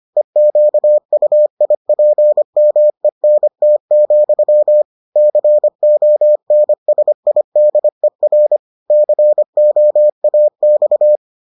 Morse code, heard as a keyed tone, about 25 wpm, 600 Hz. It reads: EQUIPMENT, CONSIDER COAX